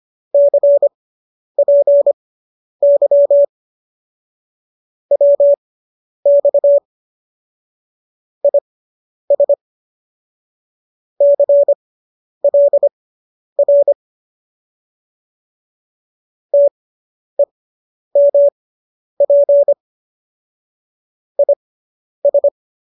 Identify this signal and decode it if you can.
Morse code, heard as a keyed tone, about 25 wpm, spaced as for 10 wpm, 570 Hz. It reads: CPY WX IS CLR TEMP IS